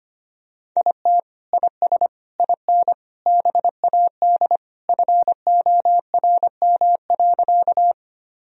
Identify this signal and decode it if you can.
Morse code, heard as a keyed tone, about 25 wpm, 700 Hz. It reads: IT IS IN BAD FORM.
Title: Sentences from Top 1000 Words - Review Part 1 - 25wpm